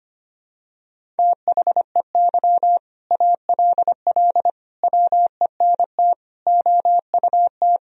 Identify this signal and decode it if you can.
Morse code, heard as a keyed tone, about 25 wpm, 700 Hz. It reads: THEY ALL WENT OUT
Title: Sentences from Top 1000 Words - Review Part 1 - 25wpm